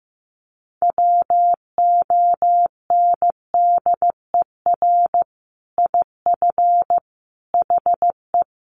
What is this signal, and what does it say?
Morse code, heard as a keyed tone, about 15 wpm, 700 Hz. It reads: WONDER IF HE
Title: Sentences from Top 500 Words 15wpm